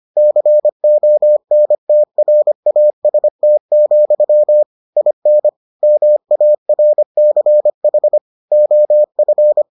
Morse code, heard as a keyed tone, about 25 wpm, 600 Hz. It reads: CONTRAST, IN MARCH OF